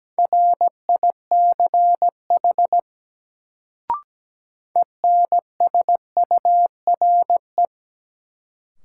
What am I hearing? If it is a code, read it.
Morse code, heard as a keyed tone, about 17 wpm, 700 Hz. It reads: RICH ENSURE
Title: Sentences from Top 1100 Words - New Word Drill Alt - 17wpm